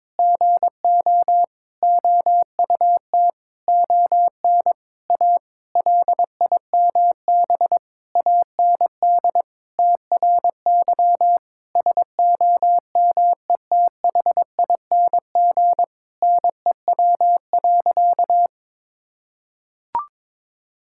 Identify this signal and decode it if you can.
Morse code, heard as a keyed tone, about 22 wpm, 700 Hz. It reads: GO OUT ON A LIMB AND TRY SOMETHING NEW.
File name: practice / 22wpm / Sentences with 8-Word Phrases - 22wpm